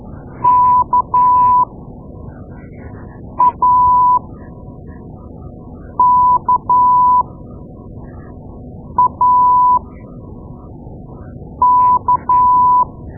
Morse code, heard as a keyed tone, about 12 wpm, spaced as for 4 wpm, 990 Hz. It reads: KAKAK
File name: Morse Lessons / Morse Lessons / Tape-2A